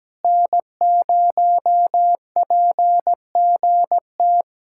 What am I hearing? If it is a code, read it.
Morse code, heard as a keyed tone, about 17 wpm, 700 Hz. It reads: N0PGT
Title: USA Callsign DE Callsign - 5 Characters - 17wpm